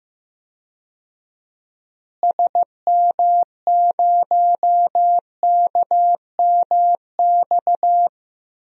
Morse code, heard as a keyed tone, about 15 wpm, 700 Hz. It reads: SM0KMX
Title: DXCC-100 Call Signs - 6 Characters - Repeated 2x - 15wpm